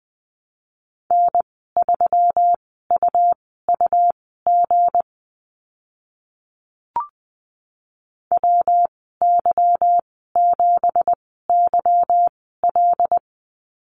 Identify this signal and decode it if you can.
Morse code, heard as a keyed tone, about 20 wpm, spaced as for 14 wpm, 700 Hz. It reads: N3UUG WY7YL
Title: 173 Lesson 28g - Number 9 - Call Signs 20wpm